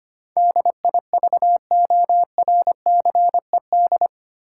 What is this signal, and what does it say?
Morse code, heard as a keyed tone, about 25 wpm, 700 Hz. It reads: DIVORCED
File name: Eight Letter Words - 25wpm